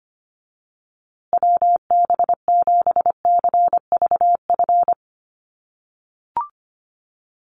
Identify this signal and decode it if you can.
Morse code, heard as a keyed tone, about 25 wpm, 700 Hz. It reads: WB7CVF
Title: USA Call Signs Repeated 2x 25wpm